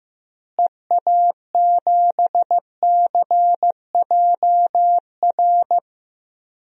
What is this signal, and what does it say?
Morse code, heard as a keyed tone, about 15 wpm, 700 Hz. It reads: EA7CJR
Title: DXCC-100 Call Signs - 6 Characters - Repeated 3x - 15wpm